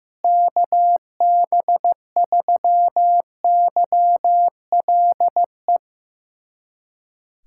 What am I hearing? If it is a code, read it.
Morse code, heard as a keyed tone, about 15 wpm, 700 Hz. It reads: KB3YLE
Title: USA Callsign DE Callsign - 6 Characters - 15wpm